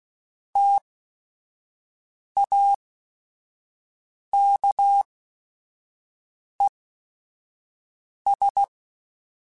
Morse code, heard as a keyed tone, about 16 wpm, spaced as for 5 wpm, 800 Hz. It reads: TAKES
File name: T then H Runs / T then H Runs.mp3s / T then H Runs @5